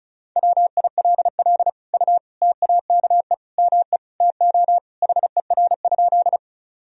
Morse code, heard as a keyed tone, about 36 wpm, 700 Hz. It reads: WILL U TAKE ME TO HER?
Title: Sentences from Top 200 Words and Abbreviations 35wpm